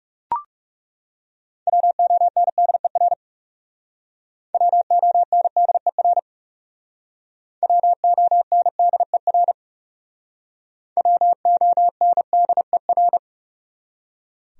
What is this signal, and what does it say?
Morse code, heard as a keyed tone, about 37 wpm, 700 Hz. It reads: WONDER WONDER WONDER WONDER
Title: Speed-Racing Top 500 Words 30wpm